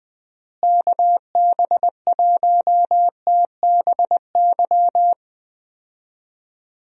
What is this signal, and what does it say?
Morse code, heard as a keyed tone, about 20 wpm, 700 Hz. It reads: KB1TBY
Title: USA Call Signs - 6 Characters - Repeated 3x - 20wpm